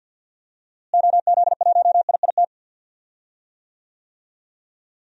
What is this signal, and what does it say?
Morse code, heard as a keyed tone, about 49 wpm, 700 Hz. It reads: OZ1IIT
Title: DXCC-100 Call Signs - 6 Characters - Repeated 3x - 50wpm